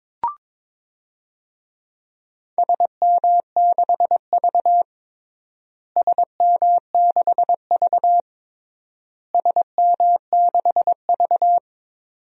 Morse code, heard as a keyed tone, about 22 wpm, 700 Hz. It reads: SM6V SM6V SM6V
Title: DXCC-100 Call Signs - 4 Characters - Repeated 3x - 22wpm